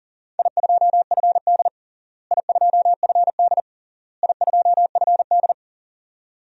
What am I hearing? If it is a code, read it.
Morse code, heard as a keyed tone, about 37 wpm, 700 Hz. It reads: I2FD I2FD I2FD
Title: DXCC-100 Call Signs - 4 Characters - Repeated 3x - 40wpm